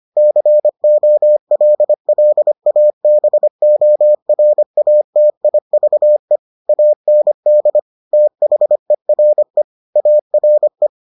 Morse code, heard as a keyed tone, about 25 wpm, 600 Hz. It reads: COLLABORATIVE AND THERE ARE